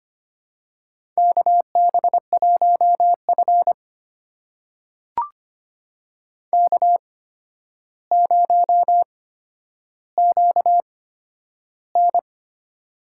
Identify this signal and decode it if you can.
Morse code, heard as a keyed tone, about 25 wpm, spaced as for 11 wpm, 700 Hz. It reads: KB1F K0QN
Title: USA Call Signs - 4 Characters - 8x Character Spacing - 25wpm